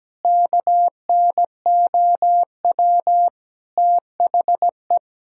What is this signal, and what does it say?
Morse code, heard as a keyed tone, about 17 wpm, 700 Hz. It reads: KNOW THE
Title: Sentences from Top 1000 Words - Review Part 6 - 17wpm